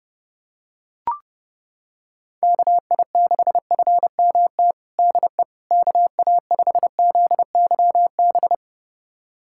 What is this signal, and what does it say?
Morse code, heard as a keyed tone, about 32 wpm, 700 Hz. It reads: KI6FMT DE KA5ZYB
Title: USA Callsign DE Callsign - 6 Characters - 30wpm